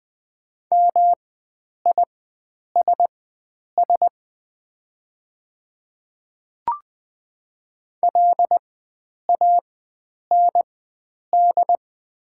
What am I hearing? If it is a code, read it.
Morse code, heard as a keyed tone, about 20 wpm, spaced as for 9 wpm, 690 Hz. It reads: MISS LAND
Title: 4x Character Spacing - Top 500 Words - 20wpm